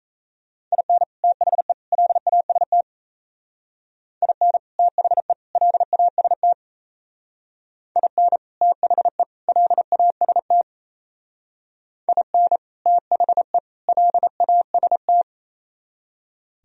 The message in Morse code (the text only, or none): IN THE LAST IN THE LAST IN THE LAST IN THE LAST